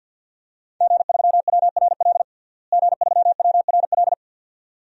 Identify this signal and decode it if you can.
Morse code, heard as a keyed tone, about 52 wpm, 700 Hz. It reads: G3WRL G3WRL
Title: DXCC-100 Call Signs - 5 Characters - Repeated 2x - 50wpm